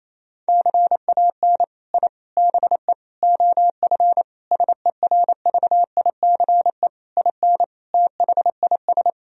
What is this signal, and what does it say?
Morse code, heard as a keyed tone, about 28 wpm, 700 Hz. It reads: CAN I BE OF SERVICE IN THIS
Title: Sentences from Top 1000 Words - Review Part 4 - 28wpm